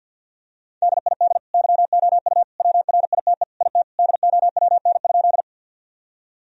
Morse code, heard as a keyed tone, about 51 wpm, 700 Hz. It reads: DID YOU WRITE IT DOWN?
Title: Sentences from Top 1000 Words - Review Part 1 - 50wpm